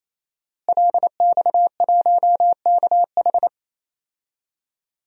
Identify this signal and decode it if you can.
Morse code, heard as a keyed tone, about 27 wpm, 700 Hz. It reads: LX1KH
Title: DXCC-100 Call Signs - 5 Characters - Repeated 3x - 28wpm